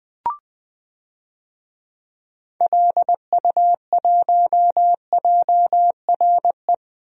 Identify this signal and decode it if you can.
Morse code, heard as a keyed tone, about 20 wpm, 700 Hz. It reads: LU1JRE
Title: DXCC-100 Call Signs - 6 Characters - Repeated 3x - 20wpm